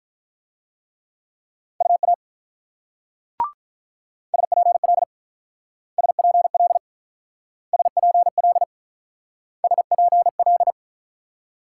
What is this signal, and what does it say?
Morse code, heard as a keyed tone, about 49 wpm, 700 Hz. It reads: UA SPL SPL SPL SPL